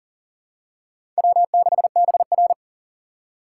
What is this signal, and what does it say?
Morse code, heard as a keyed tone, about 40 wpm, 700 Hz. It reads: W6BR
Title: USA Call Signs - 2x Character Spacing - 40wpm